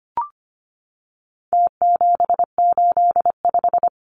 Morse code, heard as a keyed tone, about 25 wpm, 700 Hz. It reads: T785
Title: Numbers - Contest Style - 4-Digit 25wpm